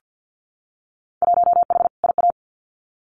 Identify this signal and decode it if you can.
Morse code, heard as a keyed tone, about 50 wpm, 700 Hz. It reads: 1H IA
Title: Winter Field Day Exchanges 50wpm